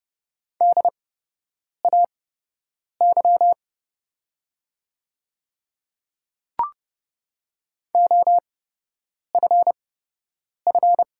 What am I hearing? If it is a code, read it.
Morse code, heard as a keyed tone, about 30 wpm, spaced as for 8 wpm, 700 Hz. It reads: DAY OFF
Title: 8x Character Spacing - Top 300 Words - 30wpm